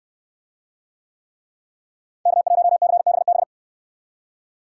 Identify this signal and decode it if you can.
Morse code, heard as a keyed tone, about 68 wpm, 700 Hz. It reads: K1CBB